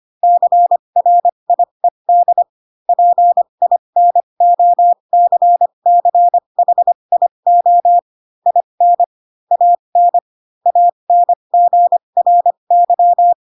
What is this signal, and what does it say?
Morse code, heard as a keyed tone, about 25 wpm, spaced as for 22 wpm, 700 Hz. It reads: CRIED PINOCCHIO IN AN ANGRY
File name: Pinocchio / Pinocchio ch7 18 25 ik1hge